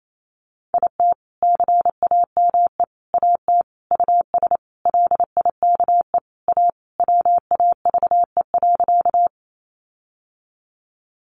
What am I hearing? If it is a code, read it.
Morse code, heard as a keyed tone, about 28 wpm, 700 Hz. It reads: IT CAME AT US LIKE A WAVE.
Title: Sentences from Top 1400 Words 28wpm